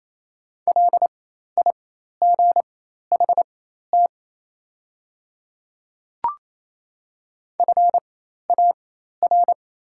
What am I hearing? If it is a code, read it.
Morse code, heard as a keyed tone, about 28 wpm, spaced as for 13 wpm, 700 Hz. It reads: LIGHT FAR
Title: 4x Character Spacing - Top 500 Words - 28wpm